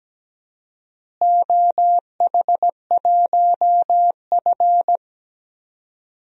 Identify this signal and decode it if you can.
Morse code, heard as a keyed tone, about 17 wpm, 700 Hz. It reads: OH1F
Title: DXCC-100 Call Signs - 4 Characters - Repeated 3x - 17wpm